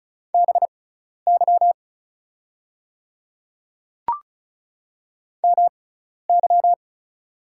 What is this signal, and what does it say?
Morse code, heard as a keyed tone, about 36 wpm, spaced as for 12 wpm, 690 Hz. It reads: BY MY
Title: 6x Character Spacing - Top 100 Words - 35wpm